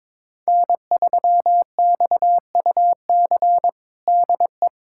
Morse code, heard as a keyed tone, about 22 wpm, 700 Hz. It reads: N3XUC DE
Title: USA Callsign DE Callsign - 5 Characters - 22wpm